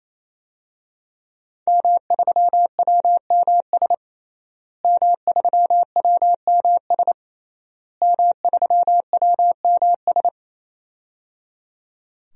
Morse code, heard as a keyed tone, about 28 wpm, 700 Hz. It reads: M3WMS M3WMS M3WMS
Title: DXCC-100 Call Signs - 5 Characters - Repeated 3x - 28wpm